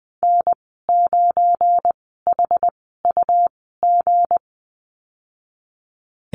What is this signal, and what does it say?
Morse code, heard as a keyed tone, about 20 wpm, spaced as for 15 wpm, 700 Hz. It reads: N9HUG
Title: 231 Lesson 37g - Letter x - Call Signs 20wpm